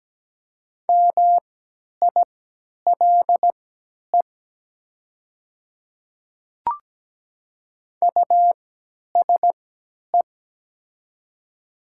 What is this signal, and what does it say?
Morse code, heard as a keyed tone, about 17 wpm, spaced as for 10 wpm, 700 Hz. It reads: MILE USE
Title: 3x Character Spacing - Top 300 Words - 17wpm